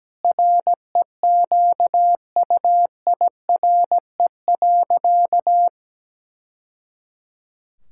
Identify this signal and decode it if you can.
Morse code, heard as a keyed tone, about 17 wpm, 700 Hz. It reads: REQUIRE.
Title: Sentences from Top 1000 Words - Review Part 6 - 17wpm